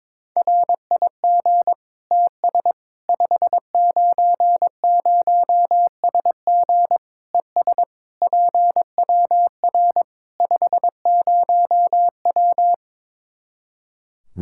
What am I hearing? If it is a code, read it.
Morse code, heard as a keyed tone, about 22 wpm, 700 Hz. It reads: RIG TS 590SG ES PWR 50W
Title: K1USN SST QSO Practice 22wpm